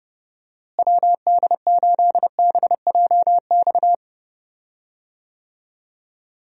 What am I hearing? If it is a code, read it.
Morse code, heard as a keyed tone, about 30 wpm, 700 Hz. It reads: WD8BJX
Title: USA Call Signs - 6 Characters - Repeated 3x - 30wpm